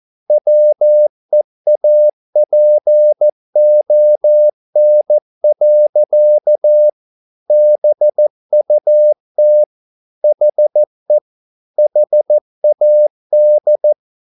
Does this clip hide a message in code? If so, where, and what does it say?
Yes: Morse code from 0.3 to 13.9 s, heard as a keyed tone, about 14 wpm, 600 Hz. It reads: WEAPON. BUT HE HAD